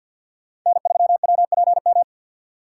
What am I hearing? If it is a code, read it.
Morse code, heard as a keyed tone, about 47 wpm, 690 Hz. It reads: N3WPM
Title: USA Call Signs - 5 Characters - 2x Character Spacing - 50wpm